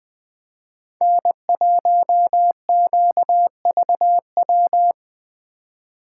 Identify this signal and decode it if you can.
Morse code, heard as a keyed tone, about 24 wpm, 700 Hz. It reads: N1QVW
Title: Speed Racing - USA 1x3 Call Signs 20wpm